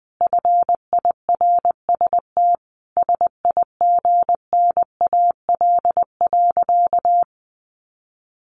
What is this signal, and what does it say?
Morse code, heard as a keyed tone, about 20 wpm, 700 Hz. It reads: FIRST SIGNAL.
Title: Sentences from Top 1400 Words 20wpm